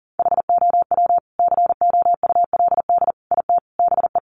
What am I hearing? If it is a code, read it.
Morse code, heard as a keyed tone, about 40 wpm, 700 Hz. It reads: HOW COULD IT BE
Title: Sentences from Top 1200 Words 40wpm